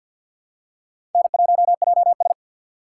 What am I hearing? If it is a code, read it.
Morse code, heard as a keyed tone, about 47 wpm, 700 Hz. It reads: N1JS
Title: USA Call Signs - 4 Characters - 6x Character Spacing - 50wpm